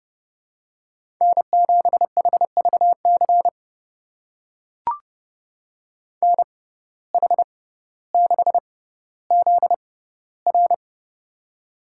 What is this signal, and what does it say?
Morse code, heard as a keyed tone, about 30 wpm, spaced as for 15 wpm, 700 Hz. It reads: N7HVC NH6ZR
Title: USA Call Signs - 5 Characters - 6x Character Spacing - 30wpm